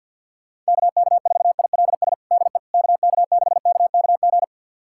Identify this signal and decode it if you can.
Morse code, heard as a keyed tone, about 52 wpm, 700 Hz. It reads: KK4ILS DE KK6KKG
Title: USA Callsign DE Callsign - 6 Characters - 50wpm